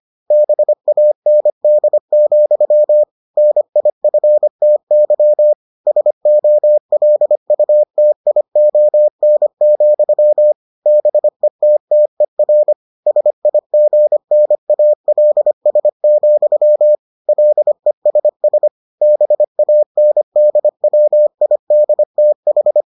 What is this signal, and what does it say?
Morse code, heard as a keyed tone, about 25 wpm, 600 Hz. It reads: BAND, NIFTY SOLUTION, BETTER SIGNALS, LESS BANDWIDTH